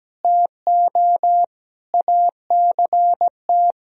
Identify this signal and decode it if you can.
Morse code, heard as a keyed tone, about 17 wpm, 700 Hz. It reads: TO ACT
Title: Sentences from Top 1000 Words - Review Part 1 - 17wpm